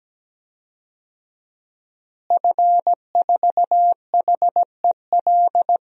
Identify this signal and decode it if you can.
Morse code, heard as a keyed tone, about 17 wpm, 700 Hz. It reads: F4HEL
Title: DXCC-100 Call Signs - 5 Characters - Repeated 3x - 17wpm